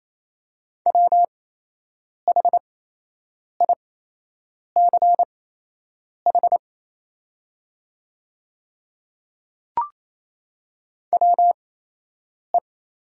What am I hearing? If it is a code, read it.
Morse code, heard as a keyed tone, about 28 wpm, spaced as for 8 wpm, 690 Hz. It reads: WHICH WE
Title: 8x Character Spacing - Top 100 Words - 28wpm